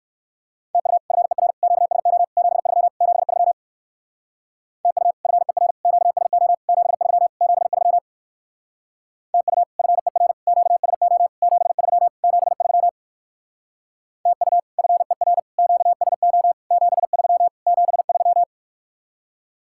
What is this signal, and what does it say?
Morse code, heard as a keyed tone, about 59 wpm, 700 Hz. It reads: TU FER QSO 73 73 TU FER QSO 73 73 TU FER QSO 73 73 TU FER QSO 73 73